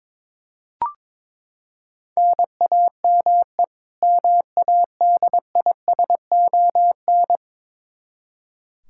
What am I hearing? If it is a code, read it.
Morse code, heard as a keyed tone, about 22 wpm, 700 Hz. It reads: NAME MADISON
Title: Practice QSO - Exchange 1 - Beginning - 22wpm